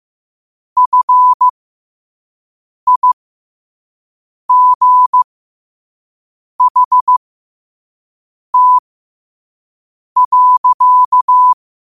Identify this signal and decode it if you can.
Morse code, heard as a keyed tone, about 15 wpm, spaced as for 5 wpm, 1000 Hz. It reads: FIGHT.
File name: qotdfiles / 5wpm